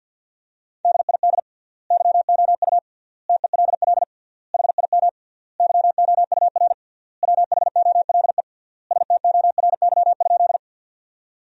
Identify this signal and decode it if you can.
Morse code, heard as a keyed tone, about 50 wpm, 700 Hz. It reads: DID YOU TELL HIM YOUR WHOLE STORY?
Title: 3x Word-Spacing - Sentences from Top 400 Words - 50wpm